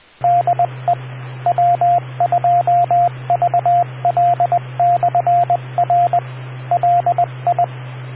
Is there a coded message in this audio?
Yes: Morse code from 0.2 to 7.6 s, heard as a keyed tone, about 20 wpm, 700 Hz, with radio noise. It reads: DE W2VL/R LI